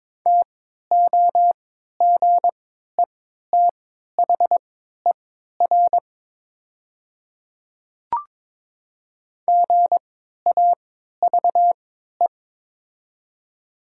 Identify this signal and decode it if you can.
Morse code, heard as a keyed tone, about 22 wpm, spaced as for 12 wpm, 700 Hz. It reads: TOGETHER GAVE
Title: 3x Character Spacing - Top 500 Words - 22wpm